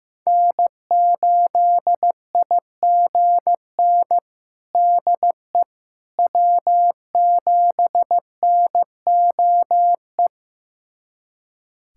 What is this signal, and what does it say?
Morse code, heard as a keyed tone, about 16 wpm, 700 Hz. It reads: N8IGN DE W7NOE
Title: USA Callsign DE Callsign - 5 Characters - 15wpm